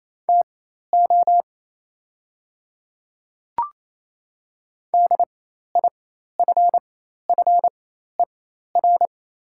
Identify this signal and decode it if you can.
Morse code, heard as a keyed tone, about 28 wpm, spaced as for 13 wpm, 700 Hz. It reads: TO DIFFER
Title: 4x Character Spacing - Top 500 Words - 28wpm